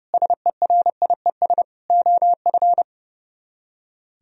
Morse code, heard as a keyed tone, about 38 wpm, 700 Hz. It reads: SERIES OF